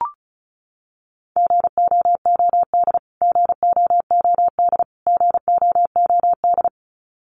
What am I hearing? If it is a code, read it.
Morse code, heard as a keyed tone, about 34 wpm, 700 Hz. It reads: GOOD GOOD GOOD
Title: Top 500 Words Repeated 35wpm